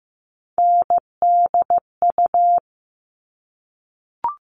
Morse code, heard as a keyed tone, about 15 wpm, 700 Hz. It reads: NDU
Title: All Cut Numbers - 3-Digit 15wpm